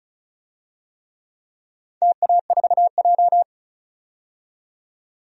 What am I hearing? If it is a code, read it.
Morse code, heard as a keyed tone, about 34 wpm, 700 Hz. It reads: TA4J
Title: DXCC-100 Call Signs - 4 Characters - 35wpm